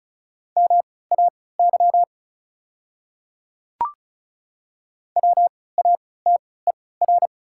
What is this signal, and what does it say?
Morse code, heard as a keyed tone, about 36 wpm, spaced as for 20 wpm, 690 Hz. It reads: MAY WATER
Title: 3x Character Spacing - Top 100 Words - 35wpm